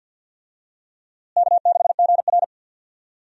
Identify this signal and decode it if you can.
Morse code, heard as a keyed tone, about 52 wpm, 700 Hz. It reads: K6GR
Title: USA Call Signs - 4 Characters - Repeated 3x - 50wpm